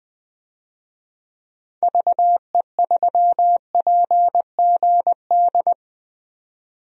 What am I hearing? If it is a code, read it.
Morse code, heard as a keyed tone, about 21 wpm, 700 Hz. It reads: VE3PGD
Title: DXCC-100 Call Signs - 6 Characters - 20wpm